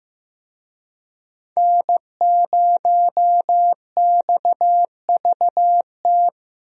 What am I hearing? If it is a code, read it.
Morse code, heard as a keyed tone, about 15 wpm, 700 Hz. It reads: N0XVT
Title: USA Call Signs - 5 Characters - Repeated 2x - 15wpm